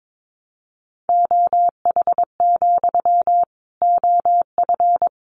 Morse code, heard as a keyed tone, about 22 wpm, 700 Hz. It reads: OH, OF
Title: Sentences from Top 500 Words 22wpm